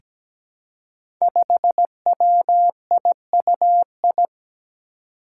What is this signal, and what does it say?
Morse code, heard as a keyed tone, about 17 wpm, 700 Hz. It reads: 5WIUI
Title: Five Challenging Letter-Number Combinations 17wpm